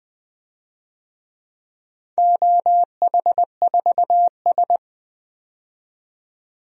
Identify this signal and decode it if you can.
Morse code, heard as a keyed tone, about 20 wpm, 700 Hz. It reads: OH4S
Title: DXCC-100 Call Signs - 4 Characters - 20wpm